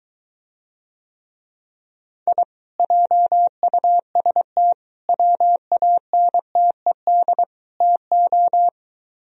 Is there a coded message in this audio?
Yes: Morse code from 2.3 to 8.7 s, heard as a keyed tone, about 23 wpm, 700 Hz. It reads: I JUST WANTED TO